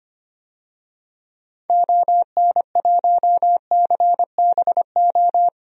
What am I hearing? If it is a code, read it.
Morse code, heard as a keyed tone, about 25 wpm, 700 Hz. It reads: ON1CBO
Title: DXCC-100 Call Signs - 6 Characters - 25wpm